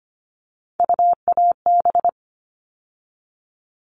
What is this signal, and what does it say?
Morse code, heard as a keyed tone, about 25 wpm, 700 Hz. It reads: UAB
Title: All Cut Numbers - 3-Digit 25wpm